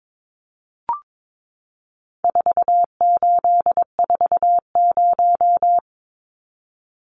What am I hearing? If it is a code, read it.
Morse code, heard as a keyed tone, about 22 wpm, 700 Hz. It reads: 4840
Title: Numbers - 4-Digit 22wpm